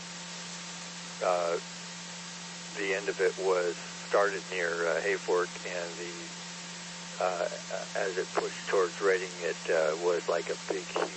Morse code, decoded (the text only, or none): none